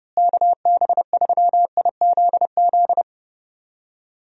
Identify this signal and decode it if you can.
Morse code, heard as a keyed tone, about 32 wpm, 700 Hz. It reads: KB3IZZ
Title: USA Call Signs - 6 Characters - Repeated 2x - 30wpm